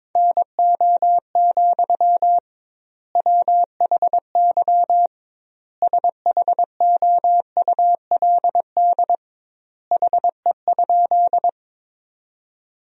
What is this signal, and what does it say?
Morse code, heard as a keyed tone, about 22 wpm, 700 Hz. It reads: NO, WHY SHOULD HE?
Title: 2x Word-Spacing - Sentences from Top 300 Words - 22wpm